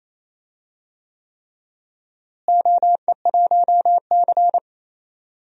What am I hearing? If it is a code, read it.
Morse code, heard as a keyed tone, about 28 wpm, 700 Hz. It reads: OE1C
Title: DXCC-100 Call Signs - 4 Characters - Repeated 3x - 28wpm